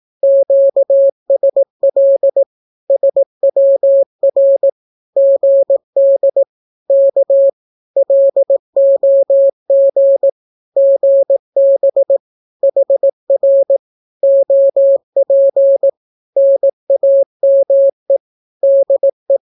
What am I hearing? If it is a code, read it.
Morse code, heard as a keyed tone, about 18 wpm, 550 Hz. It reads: QSL SWR GD K LOG GB HR OP NAME DE
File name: naukapodstaw / skroty / skroty